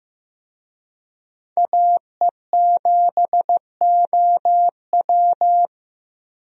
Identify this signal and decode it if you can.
Morse code, heard as a keyed tone, about 15 wpm, 700 Hz. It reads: AE7OW
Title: USA Call Signs - 5 Characters - 15wpm